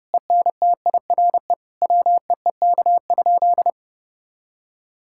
Morse code, heard as a keyed tone, about 30 wpm, 700 Hz. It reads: ENTIRE WEEK?
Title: Sentences from Top 1000 Words - Review Part 5 - 30wpm